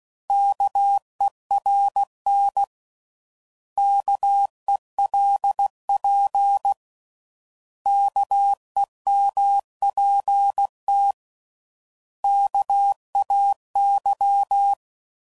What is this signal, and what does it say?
Morse code, heard as a keyed tone, about 16 wpm, 800 Hz. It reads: KERN KELP KEMPT KAY
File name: Prefixes / By First Letter / Ks